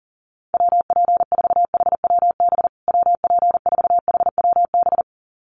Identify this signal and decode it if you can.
Morse code, heard as a keyed tone, about 40 wpm, 700 Hz. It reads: WP4HWB WP4HWB
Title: USA Call Signs Repeated 2x 40wpm